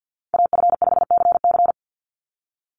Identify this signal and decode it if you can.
Morse code, heard as a keyed tone, about 50 wpm, 700 Hz. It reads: AF5CC
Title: Call Signs - 50wpm